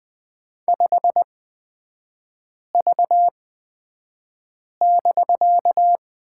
Morse code, heard as a keyed tone, about 20 wpm, spaced as for 5 wpm, 700 Hz. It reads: 5V<BK>
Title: 250 Lesson 40e - BK - Three Characters 20wpm